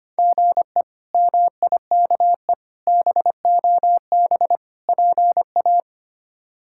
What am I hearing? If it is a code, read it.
Morse code, heard as a keyed tone, about 25 wpm, 700 Hz. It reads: GE MIKE BOB PA
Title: K1USN SST QSO Practice - 25wpm